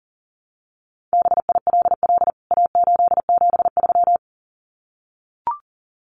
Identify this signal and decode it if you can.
Morse code, heard as a keyed tone, about 37 wpm, 700 Hz. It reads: BILL A873
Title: CWT Exchange 40wpm